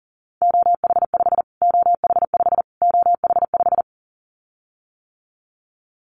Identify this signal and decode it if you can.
Morse code, heard as a keyed tone, about 40 wpm, 700 Hz. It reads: OH5 OH5 OH5
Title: Callsign Prefixes for Top 25 DXCC - Repeated 3x - 40wpm